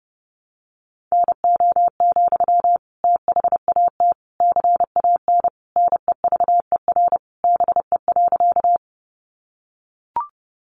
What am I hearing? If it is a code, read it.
Morse code, heard as a keyed tone, about 30 wpm, 700 Hz. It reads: NO, THAT CAN NEVER BE.
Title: Sentences from Top 300 Words 30wpm